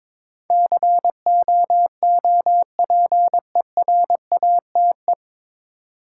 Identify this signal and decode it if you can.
Morse code, heard as a keyed tone, about 22 wpm, 700 Hz. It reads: COOPERATE
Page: Morse Code Ninja practice 22wpm